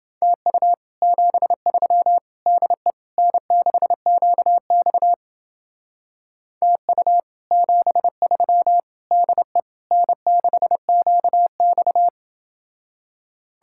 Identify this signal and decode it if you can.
Morse code, heard as a keyed tone, about 29 wpm, 700 Hz. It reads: TU 73 DE N6QX TU 73 DE N6QX